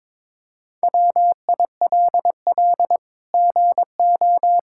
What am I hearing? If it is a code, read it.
Morse code, heard as a keyed tone, about 22 wpm, 700 Hz. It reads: WILL GO
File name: practice / 22wpm / Sets of 7 Words - 22wpm